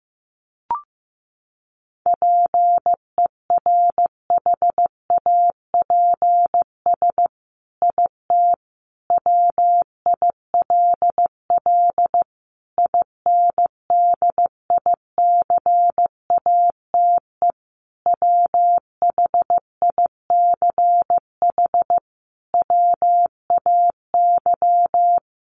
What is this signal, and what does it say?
Morse code, heard as a keyed tone, about 15 wpm, 700 Hz. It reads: PERHAPS IT WILL INDICATE WHICH WAY
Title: Sentences from Top 1300 Words 15wpm